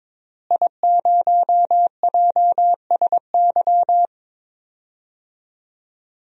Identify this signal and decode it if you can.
Morse code, heard as a keyed tone, about 22 wpm, 700 Hz. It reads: I0JSY
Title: DXCC-100 Call Signs - 5 Characters - Repeated 3x - 22wpm